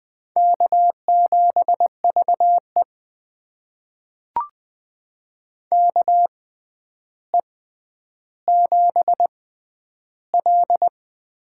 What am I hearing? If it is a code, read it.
Morse code, heard as a keyed tone, about 21 wpm, spaced as for 10 wpm, 700 Hz. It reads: K7VE KE7L
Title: USA Call Signs - 4 Characters - 6x Character Spacing - 20wpm